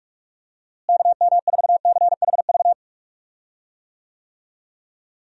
Morse code, heard as a keyed tone, about 46 wpm, 700 Hz. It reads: KM4CHV
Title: USA Call Signs - 6 Characters - Repeated 2x - 45wpm